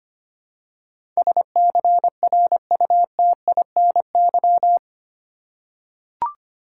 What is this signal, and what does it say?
Morse code, heard as a keyed tone, about 25 wpm, 700 Hz. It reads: SCRUTINY
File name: Eight Letter Words - 25wpm